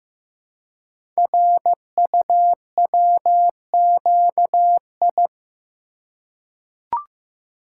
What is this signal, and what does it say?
Morse code, heard as a keyed tone, about 15 wpm, 700 Hz. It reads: RUWQI
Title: Code Groups - 5 Letters - 15wpm